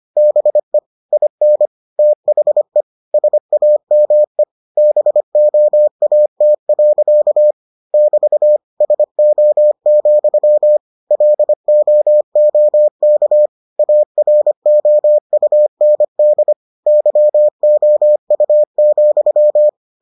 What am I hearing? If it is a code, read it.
Morse code, heard as a keyed tone, about 25 wpm, 600 Hz. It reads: BE IN THE SAME BOAT. <BT> SO, LOOK AROUND YOU,